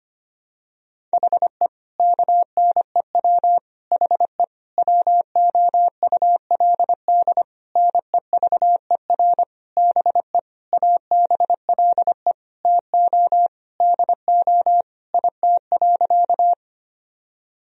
Morse code, heard as a keyed tone, about 25 wpm, 700 Hz. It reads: HE KNEW HE WOULD NEVER BE ABLE TO DO IT.
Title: Sentences from Top 1000 Words - Review Part 2 - 25wpm